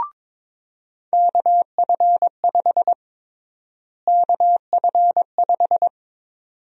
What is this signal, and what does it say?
Morse code, heard as a keyed tone, about 22 wpm, 700 Hz. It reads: KF5 KF5
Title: USA Call Sign Prefixes - Repeated 2x - 22wpm